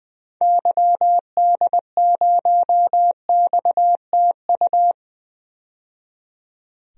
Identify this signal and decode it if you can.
Morse code, heard as a keyed tone, about 20 wpm, 700 Hz. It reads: YD0XTU
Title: DXCC-100 Call Signs - 6 Characters - 20wpm